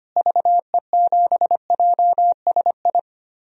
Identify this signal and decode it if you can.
Morse code, heard as a keyed tone, about 25 wpm, 700 Hz. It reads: VE7JSI
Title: DXCC-100 Call Signs - 6 Characters - Repeated 3x - 25wpm